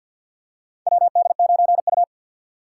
Morse code, heard as a keyed tone, about 48 wpm, 700 Hz. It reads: WD9U
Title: USA Call Signs - 4 Characters - 3x Character Spacing - 50wpm